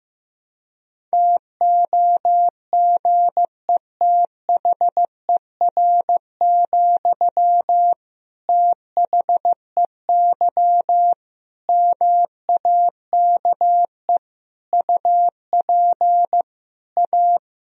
Morse code, heard as a keyed tone, about 15 wpm, 700 Hz. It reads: TOGETHER, THEY MAKE UP A
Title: Sentences from Top 1000 Words - Review Part 2 - 15wpm